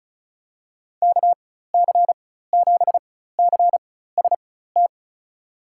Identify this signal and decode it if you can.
Morse code, heard as a keyed tone, about 34 wpm, spaced as for 16 wpm, 700 Hz. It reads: KC7CST
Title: USA Call Signs - 6 Characters - 4x Character Spacing - 35wpm